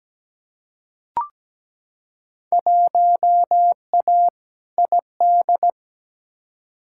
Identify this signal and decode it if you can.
Morse code, heard as a keyed tone, about 17 wpm, 700 Hz. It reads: 1A ID
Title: Field Day Exchanges - 17wpm